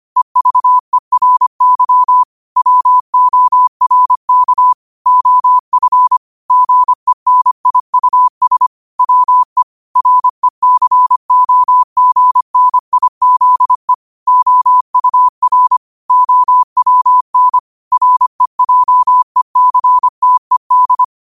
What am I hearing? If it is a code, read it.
Morse code, heard as a keyed tone, about 25 wpm, 1000 Hz. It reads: EVERY WORK OF GENIUS WE RECOGNIZE OUR OWN REJECTED
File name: qotdfiles / 25wpm